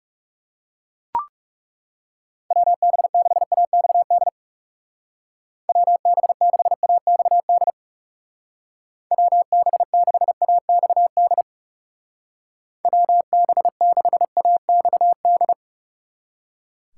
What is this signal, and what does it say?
Morse code, heard as a keyed tone, about 36 wpm, 700 Hz. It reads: WB6AXD WB6AXD WB6AXD WB6AXD
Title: Speed Racing - USA Call Signs 30wpm